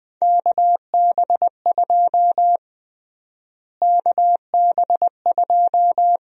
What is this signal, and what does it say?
Morse code, heard as a keyed tone, about 20 wpm, 700 Hz. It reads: KB2 KB2
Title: USA Call Sign Prefixes - Repeated 2x - 20wpm